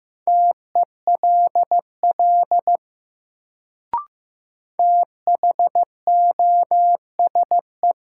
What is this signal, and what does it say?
Morse code, heard as a keyed tone, about 15 wpm, 700 Hz. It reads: TELL THOSE
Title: Sets of 2 Words - New Word Drill - 15wpm